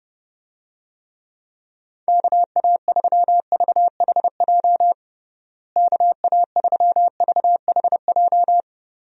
Morse code, heard as a keyed tone, about 32 wpm, 700 Hz. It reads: KA3VHJ KA3VHJ
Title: USA Call Signs - 6 Characters - Repeated 2x - 30wpm